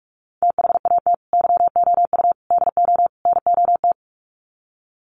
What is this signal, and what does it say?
Morse code, heard as a keyed tone, about 44 wpm, 700 Hz. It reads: THAT YOU DO NOT
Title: Sets of 4 Words 45wpm